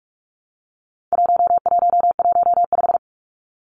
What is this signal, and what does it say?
Morse code, heard as a keyed tone, about 44 wpm, 700 Hz. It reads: 1115
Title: Numbers - 4-Digit 45wpm